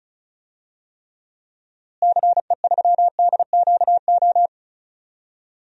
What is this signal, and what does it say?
Morse code, heard as a keyed tone, about 36 wpm, 700 Hz. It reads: CE3DQO
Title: DXCC-100 Call Signs - 6 Characters - Repeated 2x - 35wpm